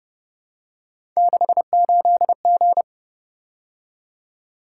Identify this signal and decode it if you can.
Morse code, heard as a keyed tone, about 29 wpm, 700 Hz. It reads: 68G
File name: Speed-Racing - Three Letter-Number Combinations - 25wpm